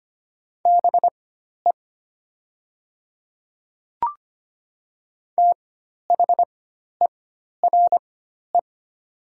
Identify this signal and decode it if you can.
Morse code, heard as a keyed tone, about 25 wpm, spaced as for 12 wpm, 700 Hz. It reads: BE THERE
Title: 4x Character Spacing - Top 100 Words - 25wpm